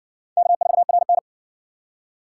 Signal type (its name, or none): Morse code